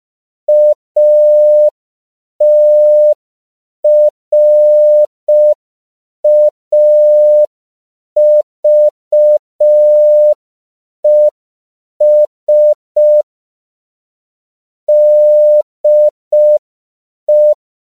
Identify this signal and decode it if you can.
Morse code, heard as a keyed tone, about 5 wpm, 600 Hz. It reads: ATRAVES DE
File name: 5PPM